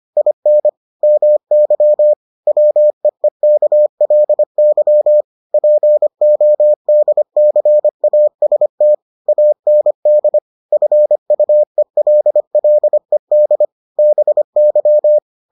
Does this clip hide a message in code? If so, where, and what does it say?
Yes: Morse code from 0.2 to 15.2 s, heard as a keyed tone, about 25 wpm, 600 Hz. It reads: IN MY WEEKLY PODCAST AND FUELLED BY